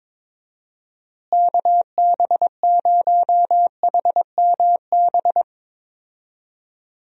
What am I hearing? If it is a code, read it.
Morse code, heard as a keyed tone, about 22 wpm, 700 Hz. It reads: KB0HMB
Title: USA Call Signs - 6 Characters - Repeated 2x - 22wpm